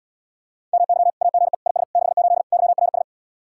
Morse code, heard as a keyed tone, about 80 wpm, 700 Hz. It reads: MY AGE IS 68 YRS